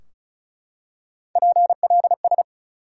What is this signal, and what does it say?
Morse code, heard as a keyed tone, about 36 wpm, 700 Hz. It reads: PLS
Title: Top 180 QSO Elements 35wpm